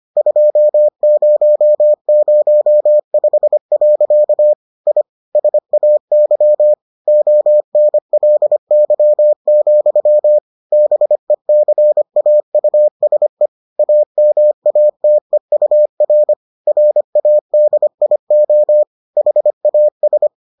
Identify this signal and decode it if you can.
Morse code, heard as a keyed tone, about 25 wpm, 600 Hz. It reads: 2005. I SAY ONLY, BECAUSE AMATEUR RADIO HAS